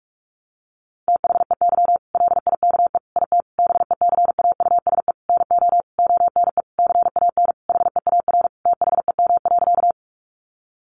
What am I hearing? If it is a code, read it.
Morse code, heard as a keyed tone, about 44 wpm, 700 Hz. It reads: THEY LIKE IT BECAUSE NO ONE CAN HEAR THEM.
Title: Sentences from Top 600 Words 45wpm